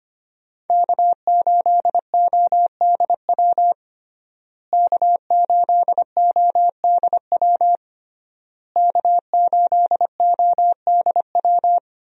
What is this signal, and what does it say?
Morse code, heard as a keyed tone, about 25 wpm, 700 Hz. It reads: K8ODW K8ODW K8ODW
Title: USA Call Signs - 5 Characters - Repeated 3x - 25wpm